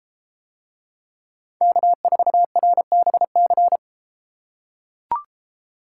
Morse code, heard as a keyed tone, about 33 wpm, 700 Hz. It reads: K4RBC